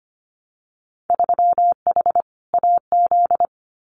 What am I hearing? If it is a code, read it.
Morse code, heard as a keyed tone, about 25 wpm, 700 Hz. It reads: 3H AZ
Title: Winter Field Day Exchanges Encore 25wpm